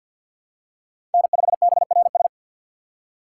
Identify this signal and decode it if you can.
Morse code, heard as a keyed tone, about 48 wpm, 700 Hz. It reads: N5BRS
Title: USA Call Signs - 5 Characters - 2x Character Spacing - 50wpm